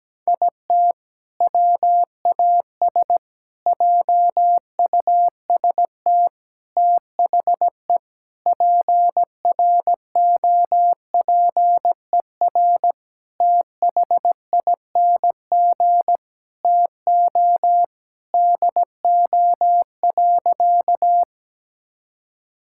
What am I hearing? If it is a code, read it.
Morse code, heard as a keyed tone, about 17 wpm, 700 Hz. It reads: IT WAS JUST THE PROPER THING TO DO.
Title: Sentences from Top 1000 Words - Review Part 5 - 17wpm